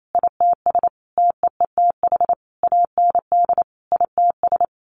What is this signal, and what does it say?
Morse code, heard as a keyed tone, about 28 wpm, 700 Hz. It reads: ITS TEETH AND ITS
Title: A Sound Of Thunder 28wpm